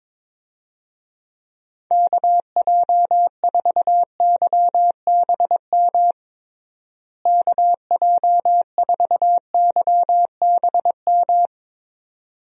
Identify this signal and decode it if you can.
Morse code, heard as a keyed tone, about 22 wpm, 700 Hz. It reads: KJ4YBM KJ4YBM
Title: USA Call Signs - 6 Characters - Repeated 2x - 22wpm